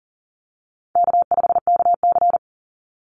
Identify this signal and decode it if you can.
Morse code, heard as a keyed tone, about 40 wpm, 700 Hz. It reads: K5KC
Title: Easy Call Signs Repeated 3x - 40wpm